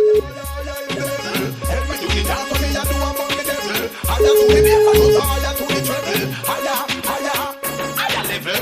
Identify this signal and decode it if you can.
Ringback tone (Europe, 425 Hz), with noise.